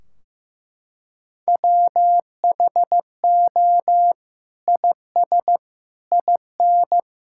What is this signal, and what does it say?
Morse code, heard as a keyed tone, about 15 wpm, 700 Hz. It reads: WHO IS IN
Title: Sentences from Top 1600 Words 15wpm